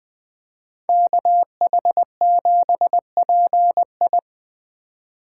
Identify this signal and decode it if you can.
Morse code, heard as a keyed tone, about 20 wpm, 700 Hz. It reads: KH7PI
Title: Speed Racing - USA 2x2 Call Signs 17wpm